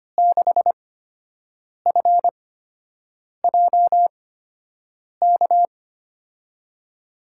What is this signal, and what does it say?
Morse code, heard as a keyed tone, about 25 wpm, spaced as for 7 wpm, 700 Hz. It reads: 6FJK
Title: USA Call Signs - 6 Characters - 8x Character Spacing - 25wpm